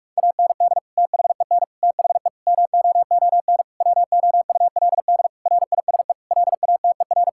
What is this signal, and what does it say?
Morse code, heard as a keyed tone, about 46 wpm, 700 Hz. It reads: AND THEN THE MOON WOULD RISE LATER
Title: Sentences from Top 1000 Words - Review Part 6 - 45wpm